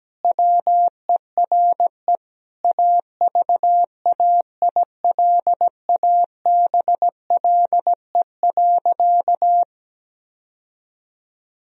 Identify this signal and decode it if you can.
Morse code, heard as a keyed tone, about 17 wpm, 700 Hz. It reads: WERE AVAILABLE.
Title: Sentences from Top 1000 Words - Review Part 3 - 17wpm